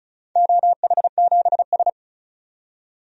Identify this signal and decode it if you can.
Morse code, heard as a keyed tone, about 36 wpm, 700 Hz. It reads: OH7S
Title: DXCC-100 Call Signs - 4 Characters - 35wpm